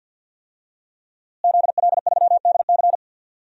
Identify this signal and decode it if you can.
Morse code, heard as a keyed tone, about 46 wpm, 700 Hz. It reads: ZL3DC
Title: DXCC-100 Call Signs - 5 Characters - Repeated 3x - 50wpm